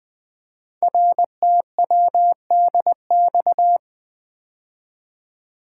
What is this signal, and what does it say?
Morse code, heard as a keyed tone, about 20 wpm, 700 Hz. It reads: RTWDX
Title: Code Groups - 5 Letters - 20wpm